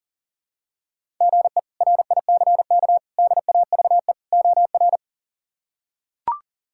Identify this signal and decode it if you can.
Morse code, heard as a keyed tone, about 40 wpm, 700 Hz. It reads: GE RICK DAVE OR
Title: K1USN SST QSO Practice - 40wpm